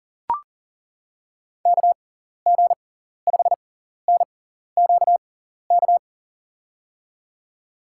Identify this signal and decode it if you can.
Morse code, heard as a keyed tone, about 40 wpm, spaced as for 14 wpm, 700 Hz. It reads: KG5NQK
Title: USA Call Signs - 6 Characters - 6x Character Spacing - 40wpm